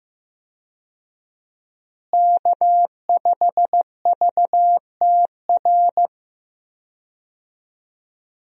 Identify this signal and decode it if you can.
Morse code, heard as a keyed tone, about 14 wpm, 700 Hz. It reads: K5VTR
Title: USA Call Signs - 5 Characters - Repeated 2x - 15wpm